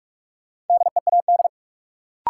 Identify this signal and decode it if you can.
Morse code, heard as a keyed tone, about 47 wpm, 700 Hz. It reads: DEAD
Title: Sentences from Top 1500 Words - New Word Drill - 45wpm